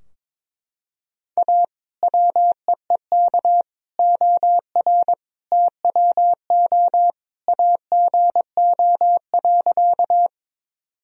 Morse code, heard as a keyed tone, about 22 wpm, 700 Hz. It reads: A WEEK OR TWO AGO.
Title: Sentences from Top 1000 Words - Review Part 3 - 22wpm